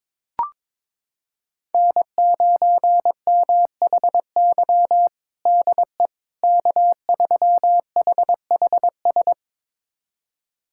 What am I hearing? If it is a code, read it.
Morse code, heard as a keyed tone, about 22 wpm, 700 Hz. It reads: N9MHY DE K3HHS
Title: USA Callsign DE Callsign - 5 Characters - 22wpm